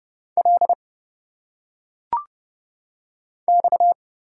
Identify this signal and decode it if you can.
Morse code, heard as a keyed tone, about 30 wpm, 700 Hz. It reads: L X
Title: Single Letters - Phonetic Pronunciation 30wpm